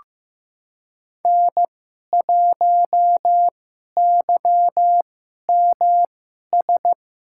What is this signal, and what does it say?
Morse code, heard as a keyed tone, about 15 wpm, spaced as for 11 wpm, 700 Hz. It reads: N1YMS
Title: USA Call Signs - 2x Character Spacing - 15wpm